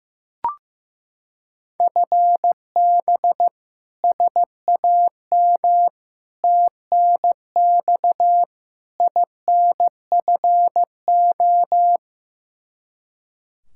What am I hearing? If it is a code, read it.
Morse code, heard as a keyed tone, about 15 wpm, 700 Hz. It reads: FB SAM TNX INFO
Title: Practice QSO - Exchange 3 - More Info - 15wpm